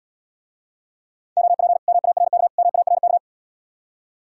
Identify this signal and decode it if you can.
Morse code, heard as a keyed tone, about 77 wpm, 700 Hz. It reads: OP MARK MARK